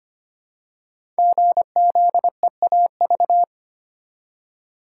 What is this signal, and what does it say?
Morse code, heard as a keyed tone, about 25 wpm, 700 Hz. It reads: GZEAV
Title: Code Groups - 5 Letter:Numbers - 25wpm